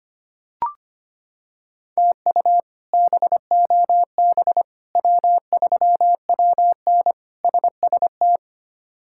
Morse code, heard as a keyed tone, about 25 wpm, 700 Hz. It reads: TU BOB W3WN SST
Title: K1USN SST QSO Practice - 25wpm